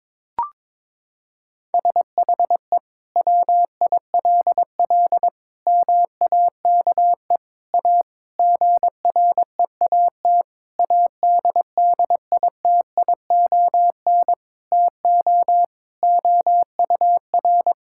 Morse code, heard as a keyed tone, about 22 wpm, 700 Hz. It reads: SHE WILL MAKE A GREAT ADDITION TO OUR
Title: Sentences from Top 1000 Words - Review Part 6 - 22wpm